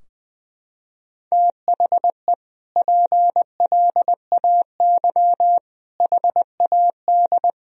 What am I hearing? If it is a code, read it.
Morse code, heard as a keyed tone, about 21 wpm, 700 Hz. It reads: THE PLAY HAD
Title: Sentences from Top 1000 Words - Review Part 4 - 20wpm